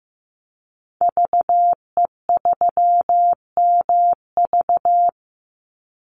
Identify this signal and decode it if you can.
Morse code, heard as a keyed tone, about 14 wpm, 700 Hz. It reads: VE3MV
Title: Call Signs Repeated 3x - 15wpm